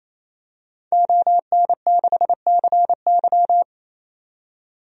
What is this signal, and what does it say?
Morse code, heard as a keyed tone, about 28 wpm, 700 Hz. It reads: ON6CY
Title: DXCC-100 Call Signs - 5 Characters - 28wpm